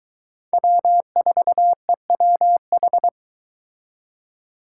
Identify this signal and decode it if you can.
Morse code, heard as a keyed tone, about 23 wpm, 700 Hz. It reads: W4EWH